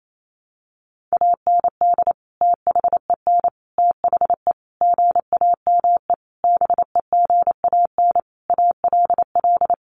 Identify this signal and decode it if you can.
Morse code, heard as a keyed tone, about 28 wpm, 700 Hz. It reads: AND THEN THE GAME BEGAN ALL
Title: Sentences from Top 500 Words 28wpm